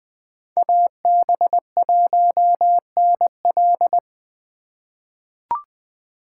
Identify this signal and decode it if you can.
Morse code, heard as a keyed tone, about 20 wpm, 700 Hz. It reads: AB1NL
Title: USA Call Signs - 5 Characters - 3x Character Spacing - 20wpm